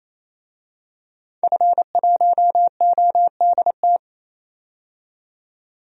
Morse code, heard as a keyed tone, about 27 wpm, 700 Hz. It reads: F1ODT
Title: DXCC-100 Call Signs - 5 Characters - 28wpm